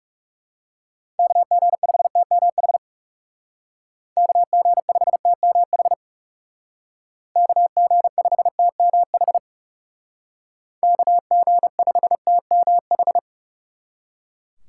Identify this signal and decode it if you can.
Morse code, heard as a keyed tone, about 37 wpm, 700 Hz. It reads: KG5TMH KG5TMH KG5TMH KG5TMH
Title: Speed Racing - USA 2x3 Call Signs 30wpm